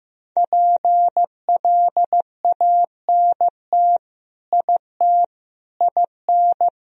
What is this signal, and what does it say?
Morse code, heard as a keyed tone, about 15 wpm, 700 Hz. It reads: PLANT IT IN
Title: Sentences from Top 1000 Words - Review Part 2 - 15wpm